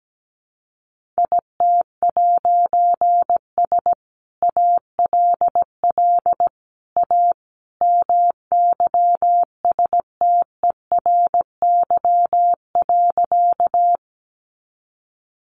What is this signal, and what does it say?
Morse code, heard as a keyed tone, about 17 wpm, 700 Hz. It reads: IT'S ALL A MYSTERY.
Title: A Sound Of Thunder 17wpm